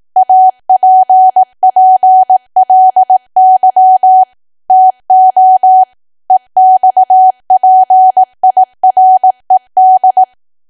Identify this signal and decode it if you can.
Morse code, heard as a keyed tone, about 18 wpm, 750 Hz. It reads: APPLY TO EXPIRED
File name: cwa / W1AW-Level2 / 18WPM